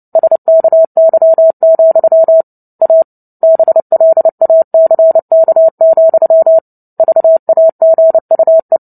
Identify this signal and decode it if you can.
Morse code, heard as a keyed tone, about 30 wpm, 650 Hz. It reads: SKY, A BLACK, VAGUE
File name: Book Two / 03-Chapter 3 - 29 WPM